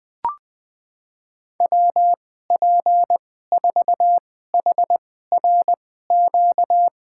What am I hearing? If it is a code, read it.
Morse code, heard as a keyed tone, about 21 wpm, spaced as for 15 wpm, 700 Hz. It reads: WP4HRQ
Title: USA Call Signs - 2x Character Spacing - 20wpm